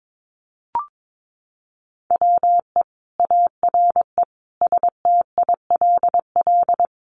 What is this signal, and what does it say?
Morse code, heard as a keyed tone, about 22 wpm, 700 Hz. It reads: WE ARE STILL
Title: Sentences from Top 1300 Words 22wpm